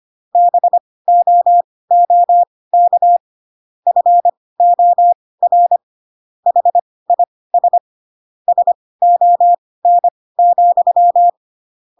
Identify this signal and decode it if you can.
Morse code, heard as a keyed tone, about 25 wpm, spaced as for 18 wpm, 700 Hz. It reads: BOOK FOR HIS SON,